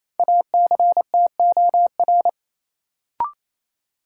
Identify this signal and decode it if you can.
Morse code, heard as a keyed tone, about 28 wpm, 700 Hz. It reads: ACTOR
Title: Sentences from Top 1500 Words - New Word Drill - 28wpm